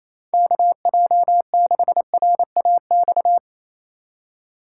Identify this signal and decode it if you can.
Morse code, heard as a keyed tone, about 28 wpm, 700 Hz. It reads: KJ6RAX
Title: USA Call Signs - 6 Characters - 2x Character Spacing - 28wpm